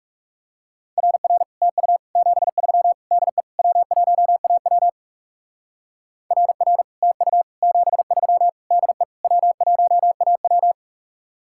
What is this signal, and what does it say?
Morse code, heard as a keyed tone, about 35 wpm, 700 Hz. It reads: RR TU 73 DE W1AW RR TU 73 DE W1AW
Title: Speed-Racing Common QSO Phrases - 30wpm